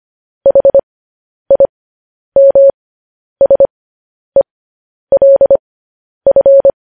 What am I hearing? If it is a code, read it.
Morse code, heard as a keyed tone, about 25 wpm, spaced as for 10 wpm, 550 Hz. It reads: HIMSELF
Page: News Headlines in Morse Code at 10 WPM